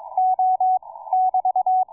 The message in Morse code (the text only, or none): O <BT>